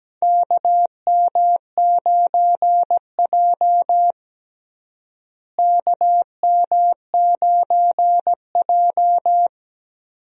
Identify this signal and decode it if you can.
Morse code, heard as a keyed tone, about 17 wpm, 700 Hz. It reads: KM9J KM9J
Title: USA Call Signs - 4 Characters - Repeated 3x - 17wpm